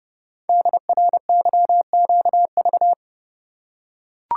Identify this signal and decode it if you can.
Morse code, heard as a keyed tone, about 30 wpm, 700 Hz. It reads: DRYQV
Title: Code Groups - 5 Letters - 30wpm